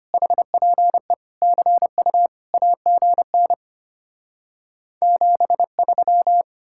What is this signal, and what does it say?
Morse code, heard as a keyed tone, about 33 wpm, 700 Hz. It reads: HPE CU AGN 73